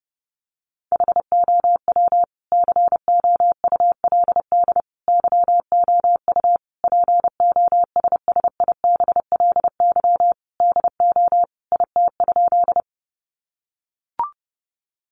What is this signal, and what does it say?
Morse code, heard as a keyed tone, about 30 wpm, 700 Hz. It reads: HOW COULD YOU POSSIBLY DO IT?
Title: Sentences from Top 1300 Words 30wpm